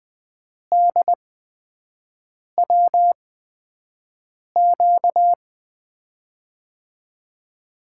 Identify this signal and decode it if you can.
Morse code, heard as a keyed tone, about 21 wpm, spaced as for 5 wpm, 700 Hz. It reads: DWQ
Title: USA Call Signs - 6 Characters - 8x Character Spacing - 20wpm